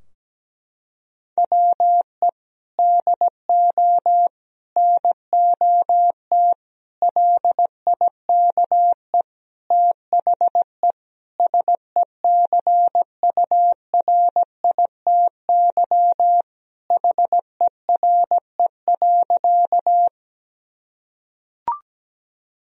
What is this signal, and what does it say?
Morse code, heard as a keyed tone, about 17 wpm, 700 Hz. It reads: WE DO NOT LIKE THE SECURITY HERE.
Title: Sentences from Top 1000 Words - Review Part 4 - 17wpm